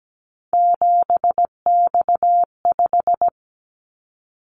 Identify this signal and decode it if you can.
Morse code, heard as a keyed tone, about 17 wpm, 700 Hz. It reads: 7X5
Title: Callsign Prefixes for Top 100 DXCC - 17wpm